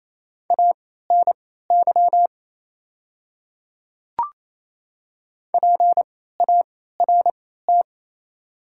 Morse code, heard as a keyed tone, about 28 wpm, spaced as for 16 wpm, 700 Hz. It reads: ANY PART
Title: 3x Character Spacing - Top 200 Words - 28wpm